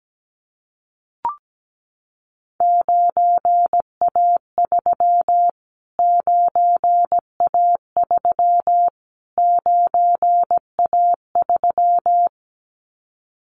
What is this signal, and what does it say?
Morse code, heard as a keyed tone, about 17 wpm, 700 Hz. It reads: 9A3 9A3 9A3
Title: Callsign Prefixes for Top 25 DXCC - Repeated 3x - 17wpm